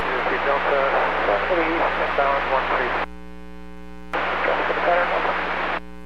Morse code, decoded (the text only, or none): none